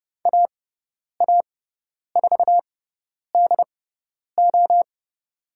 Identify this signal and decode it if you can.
Morse code, heard as a keyed tone, about 30 wpm, spaced as for 10 wpm, 700 Hz. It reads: AA4DO
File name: Lesson 17g - Letter u - Call Signs 30wpm